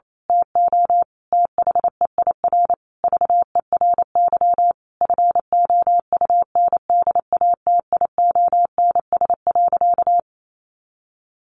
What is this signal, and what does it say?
Morse code, heard as a keyed tone, about 28 wpm, 700 Hz. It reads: TO THEIR VERY FOUNDATIONS.
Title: A Sound Of Thunder 28wpm